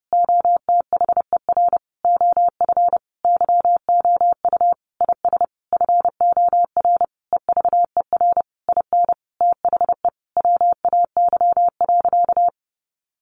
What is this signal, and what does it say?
Morse code, heard as a keyed tone, about 30 wpm, 700 Hz. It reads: OTHER OF YOU IS FOR EVER IN THE WAY.
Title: Sentences from Top 400 Words 30wpm